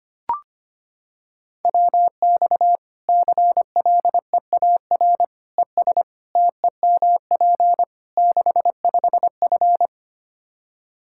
Morse code, heard as a keyed tone, about 25 wpm, 700 Hz. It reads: WX CLEAR ES TEMP 65F
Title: Practice QSO - Exchange 2 - Info - 25wpm